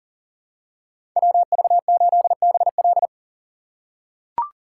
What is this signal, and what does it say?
Morse code, heard as a keyed tone, about 40 wpm, 700 Hz. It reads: WV8BL
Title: USA Call Signs - 5 Characters - 3x Character Spacing - 40wpm